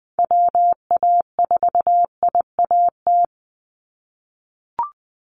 Morse code, heard as a keyed tone, about 20 wpm, 700 Hz. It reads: WA4IAT
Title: USA Call Signs 20wpm